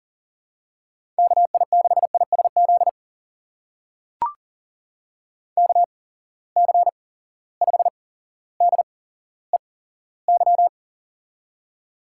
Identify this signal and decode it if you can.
Morse code, heard as a keyed tone, about 39 wpm, spaced as for 17 wpm, 700 Hz. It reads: KI6ISZ KC5DEY
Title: USA Call Signs - 6 Characters - 8x Character Spacing - 40wpm